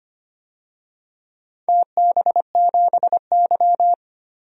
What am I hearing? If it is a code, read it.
Morse code, heard as a keyed tone, about 25 wpm, 700 Hz. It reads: TB7Y
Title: DXCC-100 Call Signs - 4 Characters - 25wpm